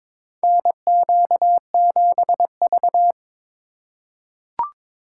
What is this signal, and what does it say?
Morse code, heard as a keyed tone, about 21 wpm, 700 Hz. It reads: NQ7V
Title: USA Call Signs - 4 Characters - 4x Character Spacing - 22wpm